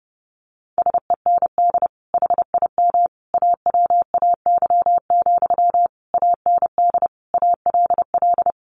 Morse code, heard as a keyed tone, about 30 wpm, 700 Hz. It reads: SEND HIM AWAY, AND ALL
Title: Sentences from Top 1500 Words 30wpm